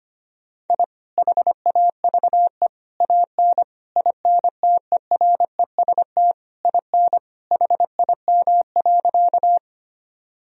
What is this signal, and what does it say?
Morse code, heard as a keyed tone, about 25 wpm, 700 Hz. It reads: I HAVE AN INTEREST IN HIM.
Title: Sentences from Top 1000 Words - Review Part 2 - 25wpm